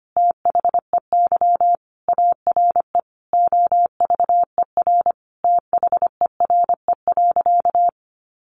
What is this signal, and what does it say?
Morse code, heard as a keyed tone, about 25 wpm, 700 Hz. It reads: THEY ARE OVER THERE.
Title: Sentences from Top 600 Words 25wpm